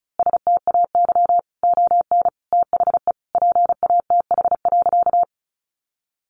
Morse code, heard as a keyed tone, about 34 wpm, 700 Hz. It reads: STAY ON THE PATH.
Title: A Sound Of Thunder 35wpm